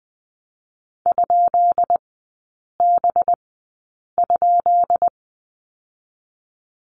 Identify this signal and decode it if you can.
Morse code, heard as a keyed tone, about 20 wpm, spaced as for 8 wpm, 700 Hz. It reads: ?B?
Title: 218 Lesson 35e - Number 0 - Three Characters 20wpm